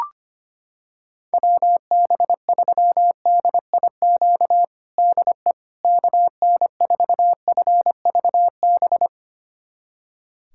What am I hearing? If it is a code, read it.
Morse code, heard as a keyed tone, about 25 wpm, 700 Hz. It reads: WB3DIQ DE KN4FVB
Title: USA Callsign DE Callsign - 6 Characters - 25wpm